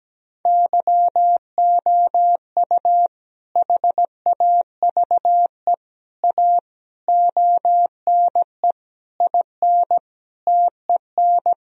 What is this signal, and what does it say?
Morse code, heard as a keyed tone, about 17 wpm, 700 Hz. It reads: YOU HAVE A ONE IN TEN
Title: Sentences from Top 1000 Words - Review Part 2 - 17wpm